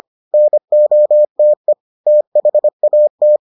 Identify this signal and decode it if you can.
Morse code, heard as a keyed tone, about 25 wpm, 600 Hz. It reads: NOTE THAT